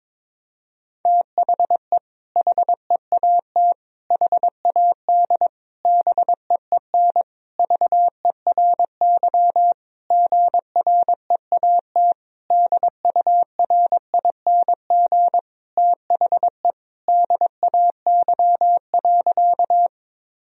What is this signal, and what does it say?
Morse code, heard as a keyed tone, about 22 wpm, 700 Hz. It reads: THE HEAT HAD BEEN VERY GREAT DURING THE DAY.
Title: Sentences from Top 1000 Words - Review Part 3 - 22wpm